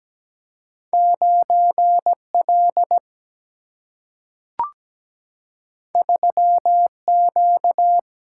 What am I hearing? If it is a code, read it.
Morse code, heard as a keyed tone, about 17 wpm, 700 Hz. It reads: 9L 3Q
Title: Single-Letter Single-Number 17wpm